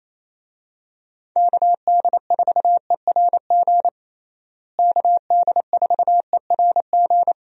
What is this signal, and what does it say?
Morse code, heard as a keyed tone, about 27 wpm, 700 Hz. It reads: KD4ERG KD4ERG
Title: USA Call Signs - 6 Characters - Repeated 2x - 28wpm